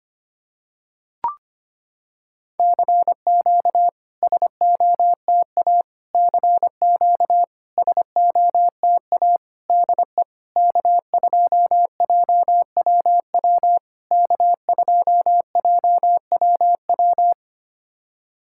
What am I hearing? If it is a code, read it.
Morse code, heard as a keyed tone, about 25 wpm, 700 Hz. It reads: CQ SOTA CQ SOTA DE K2JWW K2JWW
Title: SOTA QSO Practice 25wpm